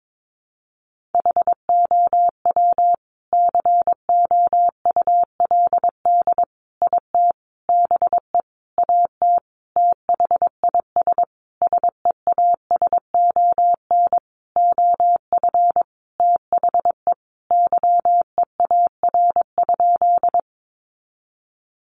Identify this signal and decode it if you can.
Morse code, heard as a keyed tone, about 22 wpm, 700 Hz. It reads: HOW COULD IT BE AT THIS SEASON OF THE YEAR?
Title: Sentences from Top 1400 Words 22wpm